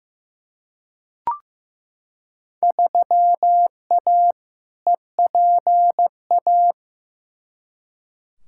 Morse code, heard as a keyed tone, about 15 wpm, 700 Hz. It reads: 3A EPA
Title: Field Day Exchanges - 15wpm